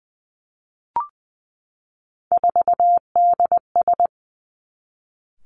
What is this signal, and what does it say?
Morse code, heard as a keyed tone, about 20 wpm, 700 Hz. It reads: 4DS
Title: Three Challenging Letter-Number Combinations 20wpm